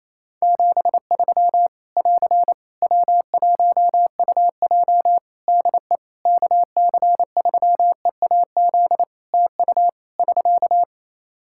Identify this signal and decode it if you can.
Morse code, heard as a keyed tone, about 27 wpm, 700 Hz. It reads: 73 <AR> W1UJ DE KC3EAZ TU <SK>
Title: Practice QSO - Exchange 4 - Ending - 28wpm